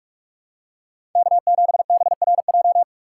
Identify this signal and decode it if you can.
Morse code, heard as a keyed tone, about 46 wpm, 700 Hz. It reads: K7BRJ
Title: USA Call Signs - 5 Characters - 6x Character Spacing - 45wpm